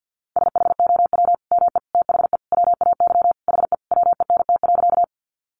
Morse code, heard as a keyed tone, about 50 wpm, 700 Hz. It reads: SHOW ME THE WAY HE WENT.
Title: Sentences from Top 200 Words 50wpm